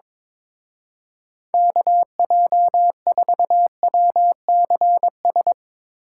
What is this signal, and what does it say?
Morse code, heard as a keyed tone, about 22 wpm, 700 Hz. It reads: KJ4WCS
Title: Speed Racing - USA 2x3 Call Signs 22wpm